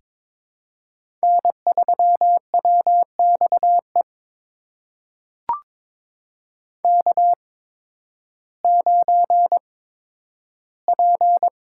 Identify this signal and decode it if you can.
Morse code, heard as a keyed tone, about 22 wpm, spaced as for 12 wpm, 700 Hz. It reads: N3WXE K9P
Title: USA Call Signs - 5 Characters - 8x Character Spacing - 22wpm